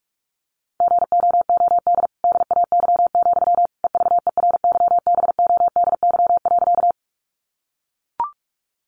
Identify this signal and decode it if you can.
Morse code, heard as a keyed tone, about 44 wpm, 700 Hz. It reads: GOOD DAY, EVERYBODY.
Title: Sentences from Top 1400 Words 45wpm